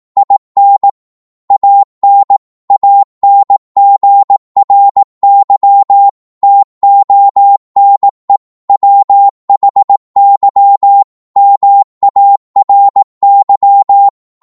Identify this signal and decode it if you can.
Morse code, heard as a keyed tone, about 18 wpm, 800 Hz. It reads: IN AN ANGRY TONE WHY MARY